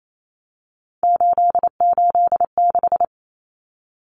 Morse code, heard as a keyed tone, about 28 wpm, 700 Hz. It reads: 886
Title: Numbers - 3-Digit 28wpm